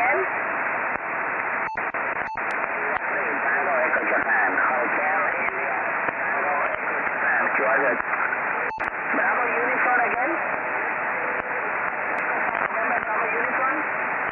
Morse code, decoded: none